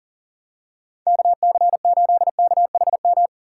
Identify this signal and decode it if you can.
Morse code, heard as a keyed tone, about 40 wpm, 700 Hz. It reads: KC8KHM
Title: USA Callsign DE Callsign - 6 Characters - 40wpm